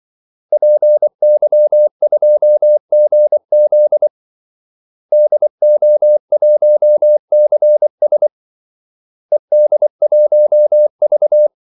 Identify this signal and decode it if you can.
Morse code, heard as a keyed tone, about 24 wpm, 600 Hz. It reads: PY2GZ DO1CS ED1V
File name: Callsigns-24wpm